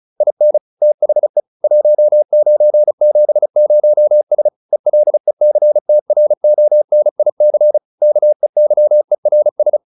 Morse code, heard as a keyed tone, about 34 wpm, 600 Hz. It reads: IN THE 1970S ELECTRONIC KEYERS